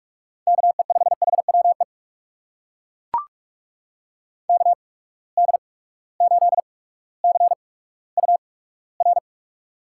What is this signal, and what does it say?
Morse code, heard as a keyed tone, about 42 wpm, spaced as for 19 wpm, 700 Hz. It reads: KE5HWE KD8CUR
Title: USA Call Signs - 6 Characters - 8x Character Spacing - 45wpm